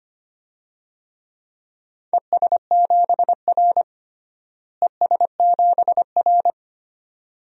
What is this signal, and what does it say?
Morse code, heard as a keyed tone, about 25 wpm, 700 Hz. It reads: ES7R ES7R
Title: DXCC-100 Call Signs - 4 Characters - Repeated 2x - 25wpm